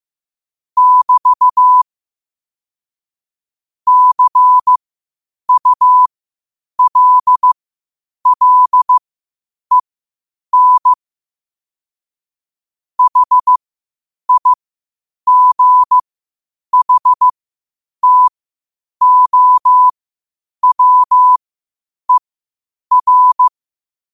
Morse code, heard as a keyed tone, about 15 wpm, spaced as for 8 wpm, 1000 Hz. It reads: <BT> CULLEN HIGHTOWER
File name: qotdfiles / 8wpm